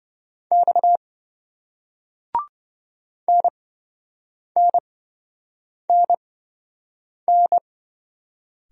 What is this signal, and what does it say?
Morse code, heard as a keyed tone, about 27 wpm, spaced as for 7 wpm, 700 Hz. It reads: X NNNN